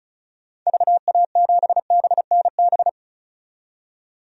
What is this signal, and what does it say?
Morse code, heard as a keyed tone, about 34 wpm, 700 Hz. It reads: VA7BNB
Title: DXCC-100 Call Signs - 6 Characters - 35wpm